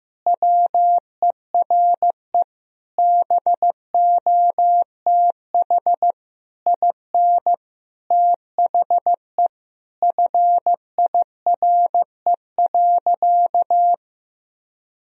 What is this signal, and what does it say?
Morse code, heard as a keyed tone, about 15 wpm, 700 Hz. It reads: WERE BOTH IN THE FIRE.
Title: Sentences from Top 1000 Words - Review Part 1 - 15wpm